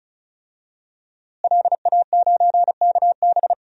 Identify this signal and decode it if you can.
Morse code, heard as a keyed tone, about 34 wpm, 700 Hz. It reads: LA9KB
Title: DXCC-100 Call Signs - 5 Characters - Repeated 3x - 35wpm